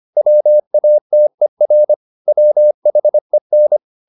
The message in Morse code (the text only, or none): WATER WHEN